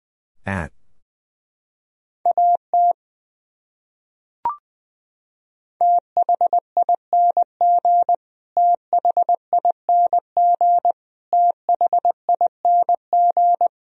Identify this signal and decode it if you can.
Morse code, heard as a keyed tone, about 20 wpm, 700 Hz. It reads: AT THING THING THING
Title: Top 100 Words Repeated 20wpm